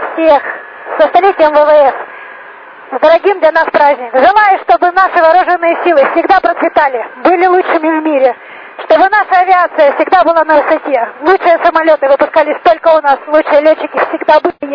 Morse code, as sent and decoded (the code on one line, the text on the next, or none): none